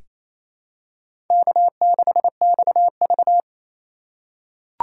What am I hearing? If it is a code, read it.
Morse code, heard as a keyed tone, about 28 wpm, 700 Hz. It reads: K6XV
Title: USA Call Signs - 4 Characters - 3x Character Spacing - 28wpm